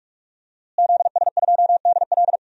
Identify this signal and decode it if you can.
Morse code, heard as a keyed tone, about 42 wpm, 700 Hz. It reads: ZS2DL
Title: DXCC-100 Call Signs - 5 Characters - Repeated 2x - 45wpm